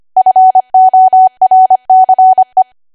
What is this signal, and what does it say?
Morse code, heard as a keyed tone, about 25 wpm, 750 Hz. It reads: FORCE
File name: Archive / 25 WPM / 25WPM